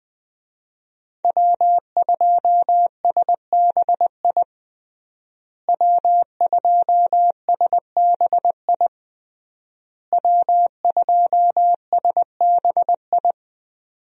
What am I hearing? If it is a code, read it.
Morse code, heard as a keyed tone, about 20 wpm, 700 Hz. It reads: W2SBI W2SBI W2SBI
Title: USA Call Signs - 5 Characters - Repeated 3x - 20wpm